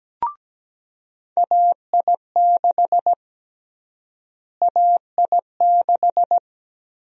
Morse code, heard as a keyed tone, about 17 wpm, 700 Hz. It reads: AI6 AI6
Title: USA Call Sign Prefixes - Repeated 2x - 17wpm